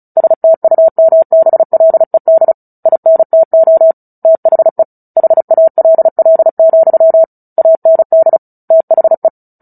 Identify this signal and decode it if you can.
Morse code, heard as a keyed tone, about 34 wpm, 650 Hz. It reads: STUMBLED INTO THE HALL, AND THE
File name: Book Two / 09-Chapter 9 - 35 WPM